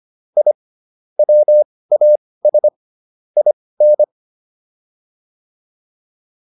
Morse code, heard as a keyed tone, about 25 wpm, spaced as for 18 wpm, 600 Hz. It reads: I WAS IN